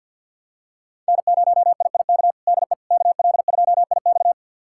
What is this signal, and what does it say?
Morse code, heard as a keyed tone, about 51 wpm, 700 Hz. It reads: N0IIK DE KL2IX
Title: USA Callsign DE Callsign - 5 Characters - 50wpm